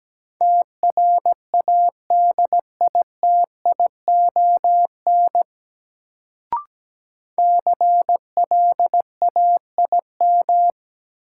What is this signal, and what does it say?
Morse code, heard as a keyed tone, about 17 wpm, 700 Hz. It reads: TRADITION CLAIM
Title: Sentences from Top 1500 Words - New Word Drill - 17wpm